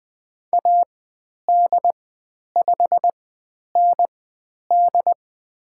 Morse code, heard as a keyed tone, about 20 wpm, spaced as for 10 wpm, 700 Hz. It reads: AD5ND
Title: 102 Lesson 17g - Letter u - Call Signs 20wpm